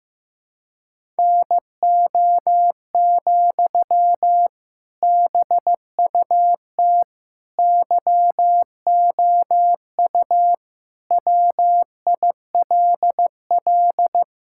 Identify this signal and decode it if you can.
Morse code, heard as a keyed tone, about 15 wpm, 700 Hz. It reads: NO, BUT YOU WILL
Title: Sentences from Top 1000 Words - Review Part 6 - 15wpm